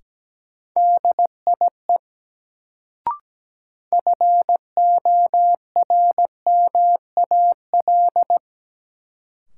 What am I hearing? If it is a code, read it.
Morse code, heard as a keyed tone, about 17 wpm, 700 Hz. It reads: DIE FORMAL
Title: Sentences from Top 1300 Words - New Word Drill - 17wpm